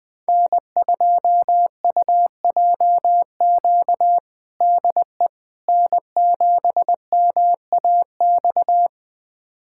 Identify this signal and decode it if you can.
Morse code, heard as a keyed tone, about 21 wpm, 700 Hz. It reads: N2UJQ DE N7MAX
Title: USA Callsign DE Callsign - 5 Characters - 20wpm